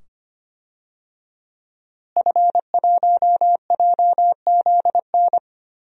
Morse code, heard as a keyed tone, about 25 wpm, 700 Hz. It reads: F1JZN
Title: DXCC-100 Call Signs - 5 Characters - Repeated 2x - 25wpm